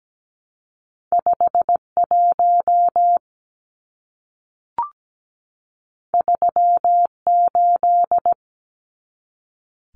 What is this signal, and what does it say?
Morse code, heard as a keyed tone, about 17 wpm, 700 Hz. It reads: 51 38
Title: Numbers - 2-Digit 17wpm